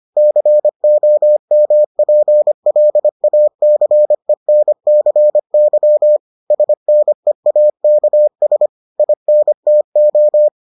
Morse code, heard as a keyed tone, about 25 wpm, 600 Hz. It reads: COMPLACENCY SNEAKS INTO